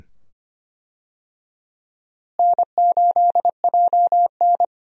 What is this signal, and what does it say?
Morse code, heard as a keyed tone, about 25 wpm, 700 Hz. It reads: N8JN
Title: USA Call Signs - 4 Characters - Repeated 2x - 25wpm